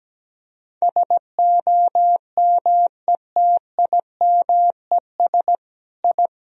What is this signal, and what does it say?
Morse code, heard as a keyed tone, about 17 wpm, 700 Hz. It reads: SOMETIMES I
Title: Sentences from Top 1000 Words - Review Part 3 - 17wpm